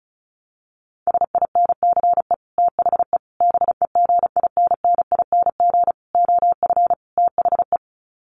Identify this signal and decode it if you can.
Morse code, heard as a keyed tone, about 34 wpm, 700 Hz. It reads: SINCE THE BEGINNING OF THE
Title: Sets of 5 Words Encore2 35wpm